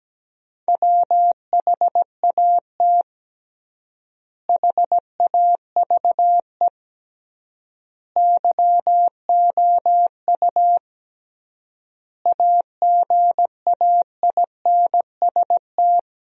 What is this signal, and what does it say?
Morse code, heard as a keyed tone, about 17 wpm, 700 Hz. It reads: WHAT HAVE YOU AGAINST
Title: 3x Word-Spacing - Sentences from Top 400 Words - 17wpm